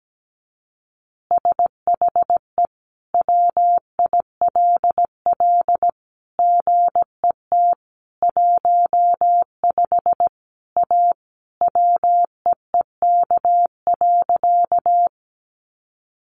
Morse code, heard as a keyed tone, about 17 wpm, 700 Hz. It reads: SHE WILL GET 15 A WEEK.
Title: Easy Phrases 17wpm